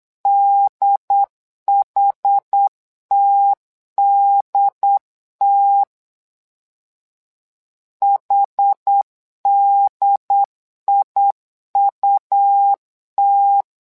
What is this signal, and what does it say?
Morse code, heard as a keyed tone, about 8 wpm, 800 Hz. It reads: DHTDT HDIUT
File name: LEZIONI / 2 UHDN / 2 UHD 8wpm 2258